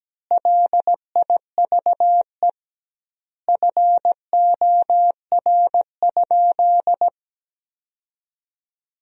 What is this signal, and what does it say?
Morse code, heard as a keyed tone, about 17 wpm, 700 Hz. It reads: LIVE FOR?
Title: 2x Word-Spacing - Sentences from Top 200 Words - 17wpm